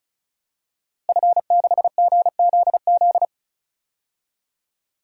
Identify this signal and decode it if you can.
Morse code, heard as a keyed tone, about 36 wpm, 700 Hz. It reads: F6GZZ
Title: DXCC-100 Call Signs - 5 Characters - Repeated 3x - 35wpm